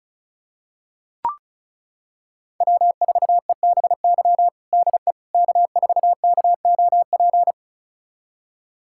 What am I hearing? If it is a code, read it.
Morse code, heard as a keyed tone, about 36 wpm, 700 Hz. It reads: W4EBY DE K4KOP
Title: USA Callsign DE Callsign - 5 Characters - 35wpm